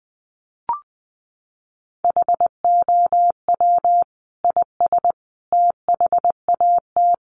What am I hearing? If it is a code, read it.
Morse code, heard as a keyed tone, about 20 wpm, 700 Hz. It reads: HOW IS THAT
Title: Sentences from Top 500 Words 20wpm